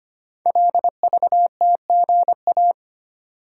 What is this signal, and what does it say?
Morse code, heard as a keyed tone, about 25 wpm, 700 Hz. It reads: LVTGA
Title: Code Groups - 5 Letters - 25wpm